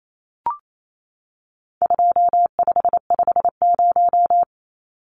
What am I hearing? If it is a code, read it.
Morse code, heard as a keyed tone, about 28 wpm, 700 Hz. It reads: 2550
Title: Numbers - 4-Digit 28wpm